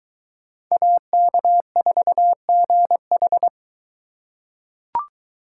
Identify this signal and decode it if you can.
Morse code, heard as a keyed tone, about 22 wpm, 700 Hz. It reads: AK4GH